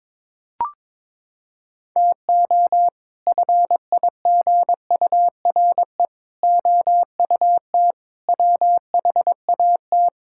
Morse code, heard as a keyed tone, about 22 wpm, 700 Hz. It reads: TO FIGURE OUT WHAT
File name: practice / 22wpm / Sets of 6 Words - 22wpm